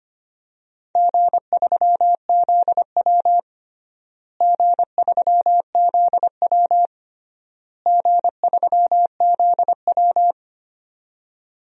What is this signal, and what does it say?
Morse code, heard as a keyed tone, about 25 wpm, 700 Hz. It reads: G3ZW G3ZW G3ZW
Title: DXCC-100 Call Signs - 4 Characters - Repeated 3x - 25wpm